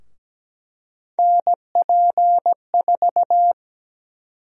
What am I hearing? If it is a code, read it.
Morse code, heard as a keyed tone, about 17 wpm, 700 Hz. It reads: NP4
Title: USA Call Sign Prefixes - Repeated 2x - 17wpm